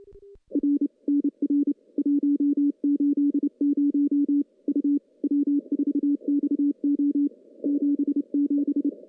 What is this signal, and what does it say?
Morse code, heard as a keyed tone, about 29 wpm, 290 Hz. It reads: R NR 180 U W4XO 77
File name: CW-SS-k5tr